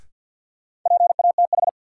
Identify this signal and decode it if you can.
Morse code, heard as a keyed tone, about 46 wpm, 700 Hz. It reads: PATH
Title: Sentences from Top 1400 Words - New Word Drill Alt - 50wpm